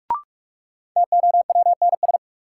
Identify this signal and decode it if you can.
Morse code, heard as a keyed tone, about 46 wpm, 700 Hz. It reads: TOWNS
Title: Sentences from Top 1000 Words - Review Part 6 - New Word Drill - 45wpm